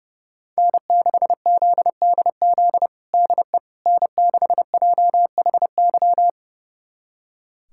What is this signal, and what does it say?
Morse code, heard as a keyed tone, about 30 wpm, 700 Hz. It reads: N6ZDZ DE N6JHY
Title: USA Callsign DE Callsign - 5 Characters - 30wpm